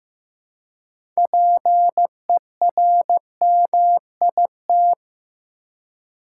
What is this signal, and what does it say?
Morse code, heard as a keyed tone, about 15 wpm, 700 Hz. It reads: PERMIT
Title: Six Letter Words - 15wpm